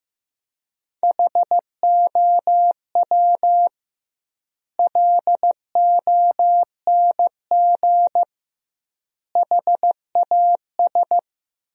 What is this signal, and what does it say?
Morse code, heard as a keyed tone, about 15 wpm, 700 Hz. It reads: HOW LONG HAS
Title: 2x Word-Spacing - Sentences from Top 100 Words - 15wpm